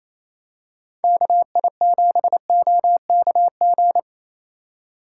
Morse code, heard as a keyed tone, about 28 wpm, 700 Hz. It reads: KI7OKG
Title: USA Call Signs - 6 Characters - 6x Character Spacing - 28wpm